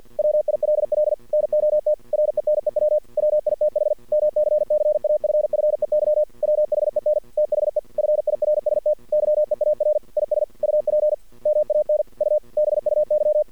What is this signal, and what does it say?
Morse code, heard as a keyed tone, about 51 wpm, 600 Hz. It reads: WILL NOT RENEW WITH MCCAFFEY WHEN THE FIRST YEAR IS UP MAN U BUY
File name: archived / CRSnet